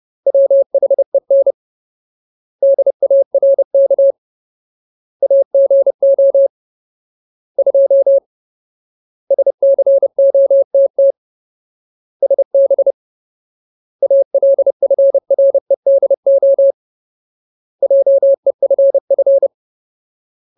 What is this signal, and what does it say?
Morse code, heard as a keyed tone, about 30 wpm, 550 Hz. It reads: WHEN DARK AGO 2 SCOTT SB ALFREDO JEFF
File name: Ultimate Word List / 30WPM Ultimate